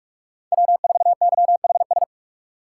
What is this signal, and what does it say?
Morse code, heard as a keyed tone, about 43 wpm, 700 Hz. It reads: W4YHS
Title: USA Call Signs - 5 Characters - 6x Character Spacing - 45wpm